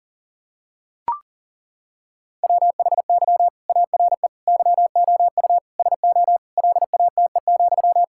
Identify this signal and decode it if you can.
Morse code, heard as a keyed tone, about 40 wpm, 700 Hz. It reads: WHY ARE YOU SO LATE,
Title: Sentences from Top 1000 Words - Review Part 6 - 40wpm